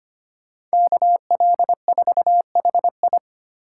Morse code, heard as a keyed tone, about 25 wpm, 700 Hz. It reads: KL4HI
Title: USA Call Signs - 5 Characters - 3x Character Spacing - 25wpm